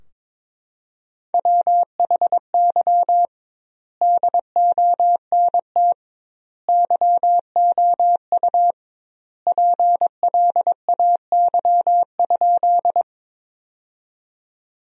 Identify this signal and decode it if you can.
Morse code, heard as a keyed tone, about 22 wpm, 700 Hz. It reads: WHY DONT YOU PLAY?
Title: 2x Word-Spacing - Sentences from Top 300 Words - 22wpm